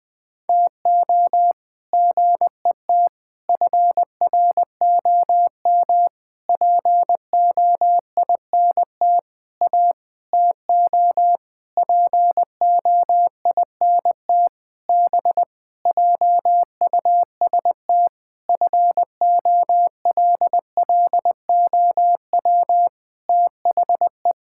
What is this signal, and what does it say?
Morse code, heard as a keyed tone, about 20 wpm, 700 Hz. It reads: TO GET FROM POINT A TO POINT B JUST FOLLOW THE
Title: Sentences with 8-Word Phrases - Spoken Beginning - 20wpm